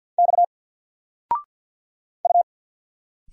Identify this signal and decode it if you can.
Morse code, heard as a keyed tone, about 50 wpm, 700 Hz. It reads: X U
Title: Single Letter-Number 50wpm